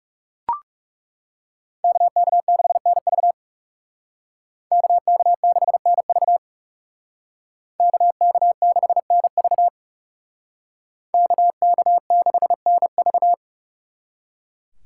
Morse code, heard as a keyed tone, about 37 wpm, 700 Hz. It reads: KK6NV KK6NV KK6NV KK6NV
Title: Speed Racing - USA 2x2 Call Signs 30wpm